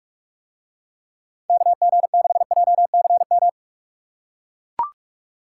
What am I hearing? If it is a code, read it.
Morse code, heard as a keyed tone, about 44 wpm, 700 Hz. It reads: KG6JCM